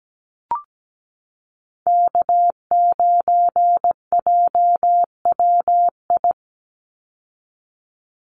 Morse code, heard as a keyed tone, about 17 wpm, 700 Hz. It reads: K9JWI
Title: Call Signs Repeated 2x - 17wpm